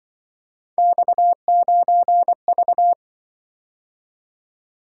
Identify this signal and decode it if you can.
Morse code, heard as a keyed tone, about 25 wpm, 700 Hz. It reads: X9V